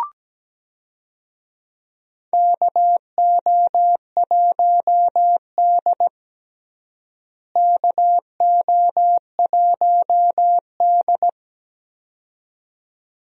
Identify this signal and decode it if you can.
Morse code, heard as a keyed tone, about 17 wpm, 700 Hz. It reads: KO1D KO1D
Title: USA Call Signs - 4 Characters - Repeated 2x - 17wpm